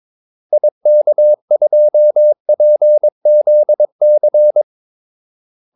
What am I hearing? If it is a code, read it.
Morse code, heard as a keyed tone, about 22 wpm, 600 Hz. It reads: IK2PZC
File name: Callsigns-22wpm